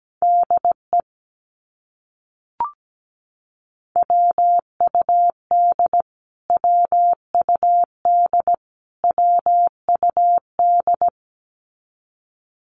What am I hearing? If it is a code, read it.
Morse code, heard as a keyed tone, about 17 wpm, 700 Hz. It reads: DE WUD WUD WUD
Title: Abbreviations Repeated 3x - 17wpm